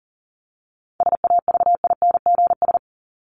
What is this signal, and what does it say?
Morse code, heard as a keyed tone, about 39 wpm, 700 Hz. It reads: SAVINGS
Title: Top 1001-1250 Words 40wpm